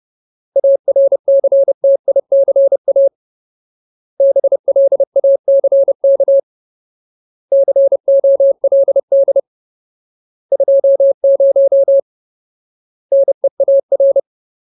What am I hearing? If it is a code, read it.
Morse code, heard as a keyed tone, about 30 wpm, 550 Hz. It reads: ARCTICA BLACK COLD 20 NEAR
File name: Ultimate Word List / 30WPM Ultimate